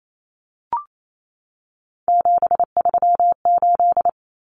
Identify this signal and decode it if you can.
Morse code, heard as a keyed tone, about 28 wpm, 700 Hz. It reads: 738
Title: Numbers - 3-Digit 28wpm